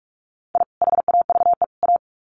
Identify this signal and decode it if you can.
Morse code, heard as a keyed tone, about 44 wpm, 700 Hz. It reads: I HAVE A
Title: Sets of 3 Words 45wpm